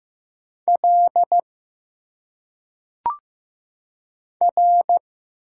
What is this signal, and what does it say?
Morse code, heard as a keyed tone, about 15 wpm, 700 Hz. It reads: L R
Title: Single Letters - Phonetic Pronunciation 15wpm